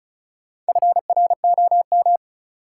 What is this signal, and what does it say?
Morse code, heard as a keyed tone, about 34 wpm, 700 Hz. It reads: FROM
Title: Sentences from Top 100 Words - New Word Drill - 35wpm